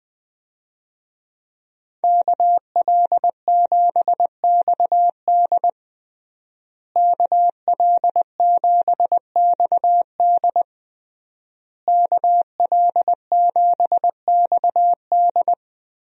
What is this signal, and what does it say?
Morse code, heard as a keyed tone, about 21 wpm, 700 Hz. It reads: KL7XD KL7XD KL7XD
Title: DXCC-100 Call Signs - 5 Characters - Repeated 3x - 20wpm